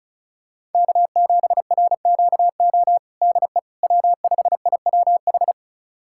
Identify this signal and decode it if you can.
Morse code, heard as a keyed tone, about 36 wpm, 700 Hz. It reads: K7RQO DE W5IWH
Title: USA Callsign DE Callsign - 5 Characters - 35wpm